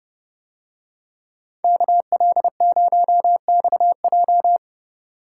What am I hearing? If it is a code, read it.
Morse code, heard as a keyed tone, about 28 wpm, 700 Hz. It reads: KL0XJ